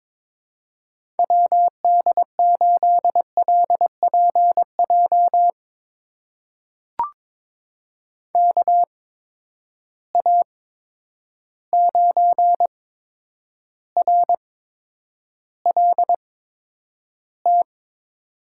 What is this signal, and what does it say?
Morse code, heard as a keyed tone, about 22 wpm, spaced as for 9 wpm, 700 Hz. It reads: WD8LPJ KA9RLT
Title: USA Call Signs - 6 Characters - 8x Character Spacing - 22wpm